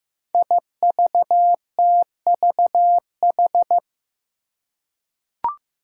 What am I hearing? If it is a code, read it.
Morse code, heard as a keyed tone, about 15 wpm, 700 Hz. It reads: IVTVH
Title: Five Challenging Letter-Number Combinations 15wpm